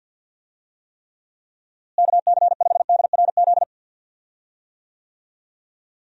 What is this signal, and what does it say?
Morse code, heard as a keyed tone, about 50 wpm, 700 Hz. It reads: KC5DRZ
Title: USA Call Signs - 6 Characters - Repeated 2x - 50wpm